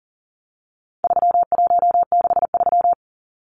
Morse code, heard as a keyed tone, about 40 wpm, 700 Hz. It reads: 3163
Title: Numbers - 4-Digit 40wpm